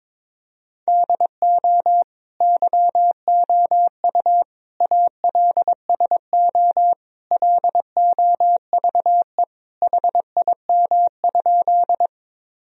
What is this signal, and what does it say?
Morse code, heard as a keyed tone, about 22 wpm, 700 Hz. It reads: DO YOU ALSO LOVE HIM?
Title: Sentences from Top 1000 Words - Review Part 1 - 22wpm